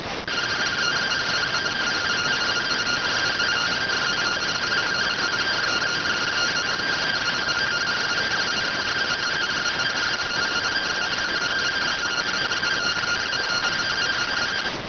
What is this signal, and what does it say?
Two-tone teleprinter signal, about 170 Hz shift.